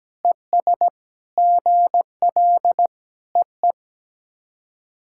Morse code, heard as a keyed tone, about 17 wpm, 700 Hz. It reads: ES GL EE
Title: SOTA QSO Practice 17wpm